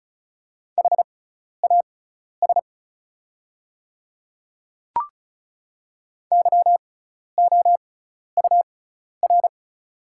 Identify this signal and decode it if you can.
Morse code, heard as a keyed tone, about 34 wpm, spaced as for 12 wpm, 700 Hz. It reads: HAS YOUR
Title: 6x Character Spacing - Top 100 Words - 35wpm